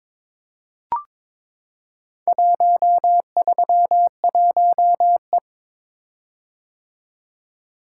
Morse code, heard as a keyed tone, about 23 wpm, 700 Hz. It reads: 131E
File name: practice / 22wpm / Four Letter-Number Combinations - 22wpm